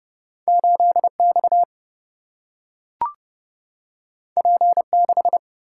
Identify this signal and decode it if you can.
Morse code, heard as a keyed tone, about 30 wpm, 700 Hz. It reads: 8X P6
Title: Single-Letter Single-Number 30wpm